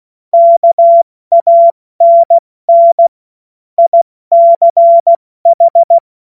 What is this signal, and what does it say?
Morse code, heard as a keyed tone, about 16 wpm, 680 Hz. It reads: KANN ICH
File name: Morsekurs / Witze / Witz 04. April 70 F80